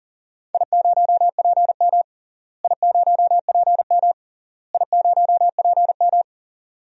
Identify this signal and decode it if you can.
Morse code, heard as a keyed tone, about 40 wpm, 700 Hz. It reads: I0PM I0PM I0PM
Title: DXCC-100 Call Signs - 4 Characters - Repeated 3x - 40wpm